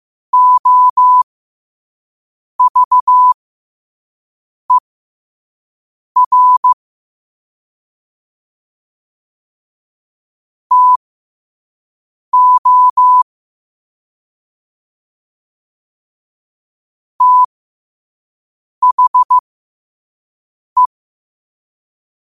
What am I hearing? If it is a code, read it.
Morse code, heard as a keyed tone, about 15 wpm, spaced as for 5 wpm, 1000 Hz. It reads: OVER TO THE
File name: qotdfiles / 5wpm